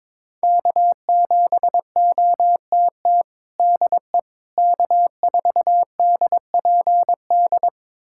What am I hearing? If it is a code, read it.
Morse code, heard as a keyed tone, about 21 wpm, 700 Hz. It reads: K7OTT DE K4DPD
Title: USA Callsign DE Callsign - 5 Characters - 22wpm